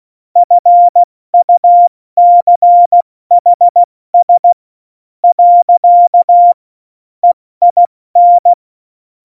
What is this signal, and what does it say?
Morse code, heard as a keyed tone, about 16 wpm, 700 Hz. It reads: FUCHS . EIN
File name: Morsekurs / Witze / Witz 28. August 2018, 70 F80